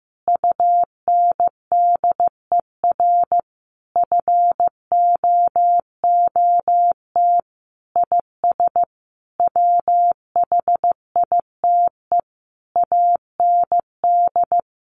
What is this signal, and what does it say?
Morse code, heard as a keyed tone, about 15 wpm, 700 Hz. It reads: UNDER FOOT IS WHITE AND
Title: Sentences from Top 1500 Words 15wpm